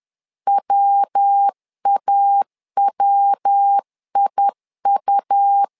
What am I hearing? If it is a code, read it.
Morse code, heard as a keyed tone, about 10 wpm, 800 Hz. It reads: WAWIU
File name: LEZIONI / 3 MWRO / 3 MW 10wpm 1959